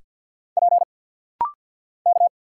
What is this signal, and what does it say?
Morse code, heard as a keyed tone, about 50 wpm, 700 Hz. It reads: P K
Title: Single Letters - Rapid-Fire 50wpm